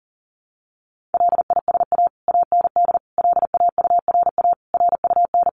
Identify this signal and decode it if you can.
Morse code, heard as a keyed tone, about 40 wpm, 700 Hz. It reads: LISA AND LAURA RUN
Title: Additional Easy Phrases 40wpm